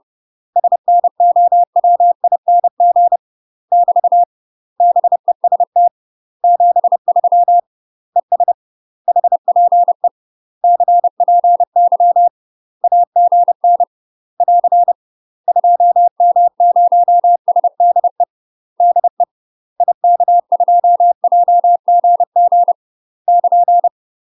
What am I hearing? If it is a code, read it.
Morse code, heard as a keyed tone, about 30 wpm, 700 Hz. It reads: SNOWING <BT> BEST 73 ES HPE CPY AGN <AR> 2M0SDE DE IK2JGG <KN>
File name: UK MASTER QSOs TRAINING / 30WPM / UK MASTER QSOs 0001